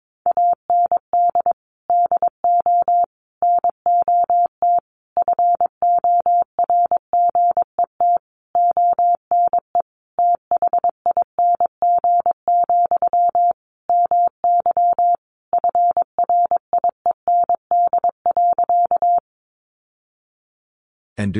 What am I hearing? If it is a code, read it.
Morse code, heard as a keyed tone, about 22 wpm, 700 Hz. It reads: AND DO NOT FORGET ONE THING, MY FRIEND.
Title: Sentences from Top 1200 Words 22wpm